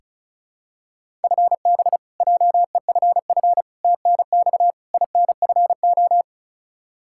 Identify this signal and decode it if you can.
Morse code, heard as a keyed tone, about 36 wpm, 700 Hz. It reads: FB JEFF TNX INFO
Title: Practice QSO - Exchange 3 - More Info - 35wpm